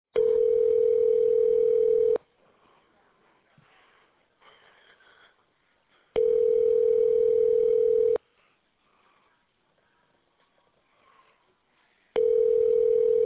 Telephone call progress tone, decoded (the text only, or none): ringback tone